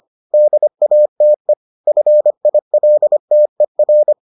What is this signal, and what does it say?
Morse code, heard as a keyed tone, about 25 wpm, 600 Hz. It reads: DATE FILTER